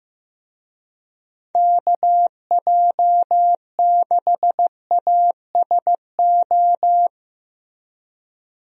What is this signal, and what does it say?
Morse code, heard as a keyed tone, about 15 wpm, 700 Hz. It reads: KJ6ASO
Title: USA Call Signs - 6 Characters - Repeated 3x - 15wpm